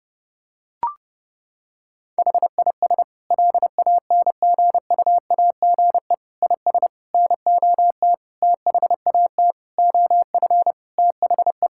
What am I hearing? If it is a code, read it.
Morse code, heard as a keyed tone, about 30 wpm, 700 Hz. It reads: HIS LANGUAGE IS NOT THAT OF THE
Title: Sentences from Top 1600 Words 30wpm